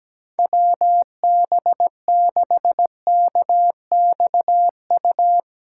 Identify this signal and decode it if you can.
Morse code, heard as a keyed tone, about 17 wpm, 700 Hz. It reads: WB6KXU
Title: USA Callsign DE Callsign 17wpm